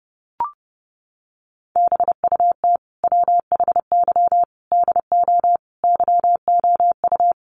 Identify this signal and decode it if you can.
Morse code, heard as a keyed tone, about 30 wpm, 700 Hz. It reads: BUT WHY DO YOU
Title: Sentences from Top 200 Words 30wpm